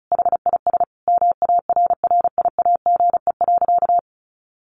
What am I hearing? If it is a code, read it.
Morse code, heard as a keyed tone, about 34 wpm, 700 Hz. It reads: HIS MARRIAGE.
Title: Sentences from Top 1400 Words 35wpm